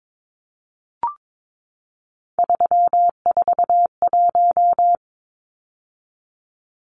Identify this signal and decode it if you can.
Morse code, heard as a keyed tone, about 22 wpm, 700 Hz. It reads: 341
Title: Numbers - 3-Digit 22wpm